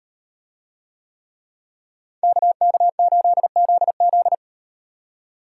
Morse code, heard as a keyed tone, about 36 wpm, 700 Hz. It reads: KK8ZZ